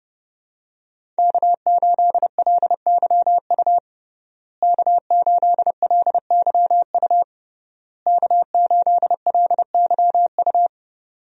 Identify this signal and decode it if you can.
Morse code, heard as a keyed tone, about 32 wpm, 700 Hz. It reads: K8LYU K8LYU K8LYU
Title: USA Call Signs - 5 Characters - Repeated 3x - 30wpm